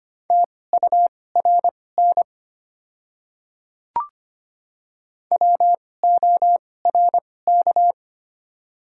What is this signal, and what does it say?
Morse code, heard as a keyed tone, about 25 wpm, spaced as for 18 wpm, 700 Hz. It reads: TURN WORK
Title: 2x Character Spacing - Top 200 Words - 25wpm